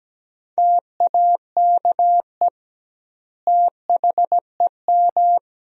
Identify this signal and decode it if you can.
Morse code, heard as a keyed tone, about 17 wpm, 700 Hz. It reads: TAKE THEM
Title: 2x Word-Spacing - Sentences from Top 200 Words - 17wpm